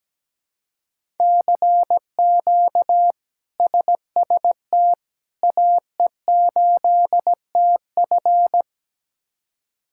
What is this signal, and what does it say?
Morse code, heard as a keyed tone, about 17 wpm, 700 Hz. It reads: CQ SST AE8TF
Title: K1USN SST QSO Practice - 17wpm